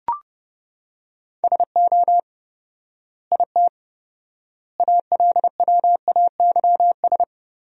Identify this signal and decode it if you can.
Morse code, heard as a keyed tone, about 32 wpm, 700 Hz. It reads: SO IT ALWAYS
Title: 4x Word-Spacing - Sentences from Top 300 Words - 30wpm